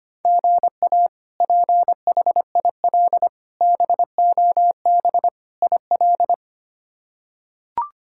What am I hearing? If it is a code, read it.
Morse code, heard as a keyed tone, about 25 wpm, 700 Hz. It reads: GA PHIL BOB IL
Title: K1USN SST QSO Practice - 25wpm